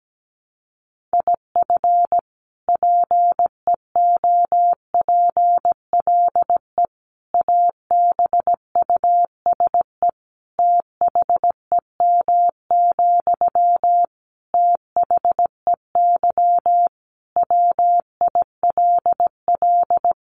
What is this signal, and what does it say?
Morse code, heard as a keyed tone, about 17 wpm, 700 Hz. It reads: IF PEOPLE ABUSE THEM, THEY WILL
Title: Sentences from Top 1300 Words 17wpm